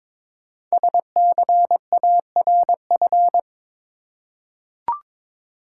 Morse code, heard as a keyed tone, about 22 wpm, 700 Hz. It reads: SCARF
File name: Difficult Words with Hints - 22wpm